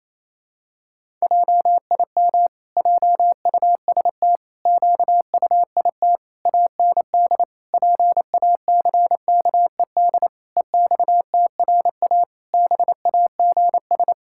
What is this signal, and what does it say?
Morse code, heard as a keyed tone, about 27 wpm, 700 Hz. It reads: JIM JUST QUIT AND PACKED EXTRA BAGS
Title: Pangram Challenge v2 - 28wpm